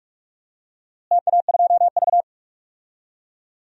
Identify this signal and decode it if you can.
Morse code, heard as a keyed tone, about 46 wpm, 700 Hz. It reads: TA2V
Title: DXCC-100 Call Signs - 4 Characters - Repeated 3x - 45wpm